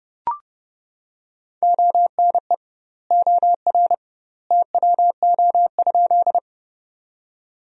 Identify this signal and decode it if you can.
Morse code, heard as a keyed tone, about 30 wpm, 700 Hz. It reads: ONE OR TWO?
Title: 2x Word-Spacing - Sentences from Top 100 Words - 30wpm